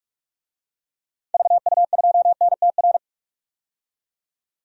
Morse code, heard as a keyed tone, about 45 wpm, 700 Hz. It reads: VU2NTR
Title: DXCC-100 Call Signs - 6 Characters - 45wpm